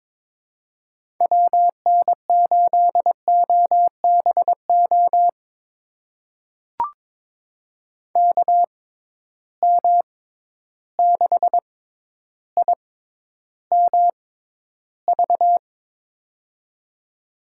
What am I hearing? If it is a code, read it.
Morse code, heard as a keyed tone, about 22 wpm, spaced as for 11 wpm, 700 Hz. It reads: WN8OBO KM6IMV
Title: USA Call Signs - 6 Characters - 6x Character Spacing - 22wpm